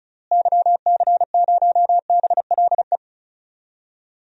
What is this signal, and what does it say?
Morse code, heard as a keyed tone, about 36 wpm, 700 Hz. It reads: YC0BLE
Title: DXCC-100 Call Signs - 6 Characters - Repeated 3x - 35wpm